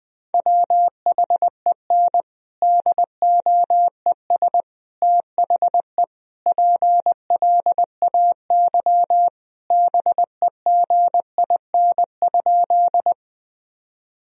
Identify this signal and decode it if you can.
Morse code, heard as a keyed tone, about 20 wpm, 700 Hz. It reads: WHEN DOES THE PLAY BEGIN?
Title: Sentences from Top 1000 Words - Review Part 1 - 20wpm